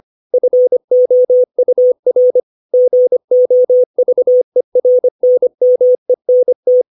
Morse code, heard as a keyed tone, about 25 wpm, 500 Hz. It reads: FOUR GOVERNMENT
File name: 632 words / 632 Words 25WPM